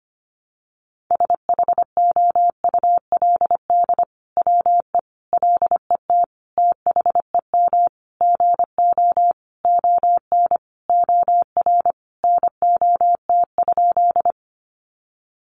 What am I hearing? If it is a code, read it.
Morse code, heard as a keyed tone, about 25 wpm, 700 Hz. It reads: SHOULD WE LET THEM GO ON OR NOT?
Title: Sentences from Top 600 Words 25wpm